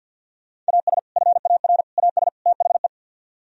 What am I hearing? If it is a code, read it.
Morse code, heard as a keyed tone, about 64 wpm, spaced as for 56 wpm, 700 Hz. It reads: AS FAR AS THE
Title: Speed-Racing - Sets of 4 Words 50wpm